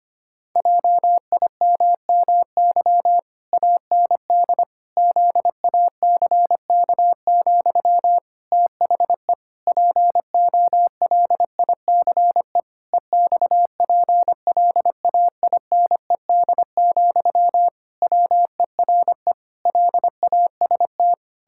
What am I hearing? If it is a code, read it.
Morse code, heard as a keyed tone, about 25 wpm, 700 Hz. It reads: JIMMY AND ZACK, THE POLICE EXPLAINED, WERE LAST
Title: Pangram Challenge v2 - 25wpm